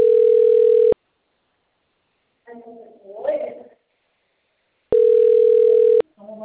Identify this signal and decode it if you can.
Ringback tone (North America), 440 + 480 Hz, with speech.
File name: out- - - - - .30789